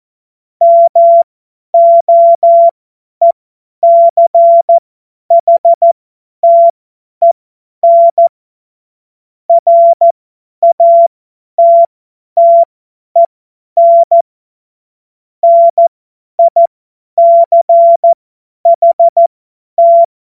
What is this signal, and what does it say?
Morse code, heard as a keyed tone, about 14 wpm, spaced as for 10 wpm, 680 Hz. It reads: MOECHTEN RATTEN NICHT
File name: Morsekurs / Witze / Witz 02. August 2022 50 F70